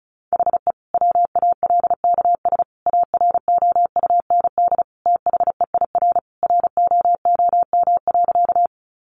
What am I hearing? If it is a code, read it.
Morse code, heard as a keyed tone, about 34 wpm, 700 Hz. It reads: HE WALKS AROUND THEIR ROOM.
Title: Easy Phrases 35wpm